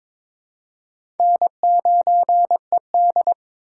Morse code, heard as a keyed tone, about 22 wpm, 700 Hz. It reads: N9ED
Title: USA Call Signs - 4 Characters - 4x Character Spacing - 22wpm